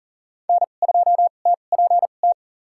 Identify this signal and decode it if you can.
Morse code, heard as a keyed tone, about 40 wpm, spaced as for 29 wpm, 700 Hz. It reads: N2TPT
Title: USA Call Signs - 5 Characters - 2x Character Spacing - 40wpm